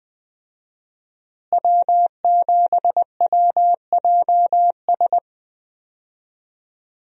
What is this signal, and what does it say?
Morse code, heard as a keyed tone, about 20 wpm, 700 Hz. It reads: W7WJS
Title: USA Call Signs - 5 Characters - Repeated 3x - 20wpm